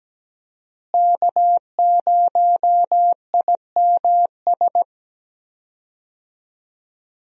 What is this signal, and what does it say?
Morse code, heard as a keyed tone, about 17 wpm, 700 Hz. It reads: K0IMS
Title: USA Call Signs - 5 Characters - Repeated 2x - 17wpm